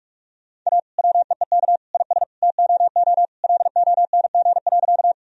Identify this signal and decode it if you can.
Morse code, heard as a keyed tone, about 46 wpm, 700 Hz. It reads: A WEEK IS TOO LONG.
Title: Sentences from Top 1000 Words - Review Part 2 - 45wpm